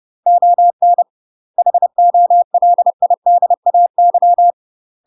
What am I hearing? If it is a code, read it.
Morse code, heard as a keyed tone, about 30 wpm, 700 Hz. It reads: ON HOLIDAY
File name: UK MASTER QSOs TRAINING / 30WPM / UK MASTER QSOs 0009